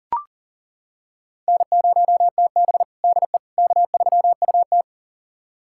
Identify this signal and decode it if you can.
Morse code, heard as a keyed tone, about 40 wpm, 700 Hz. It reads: N0TB DE K3UT
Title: USA Callsign DE Callsign - 4 Characters - 40wpm